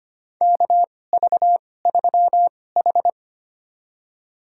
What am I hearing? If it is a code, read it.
Morse code, heard as a keyed tone, about 25 wpm, spaced as for 18 wpm, 700 Hz. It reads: KV3H
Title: USA Call Signs - 4 Characters - 2x Character Spacing - 25wpm